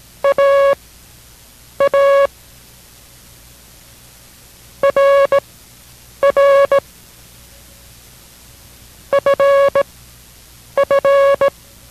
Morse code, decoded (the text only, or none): AA RR FF